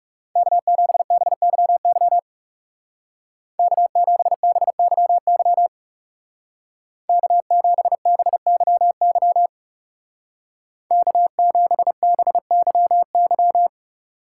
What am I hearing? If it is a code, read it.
Morse code, heard as a keyed tone, about 38 wpm, 700 Hz. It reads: K7BYY K7BYY K7BYY K7BYY
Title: Speed Racing - USA 1x3 Call Signs 30wpm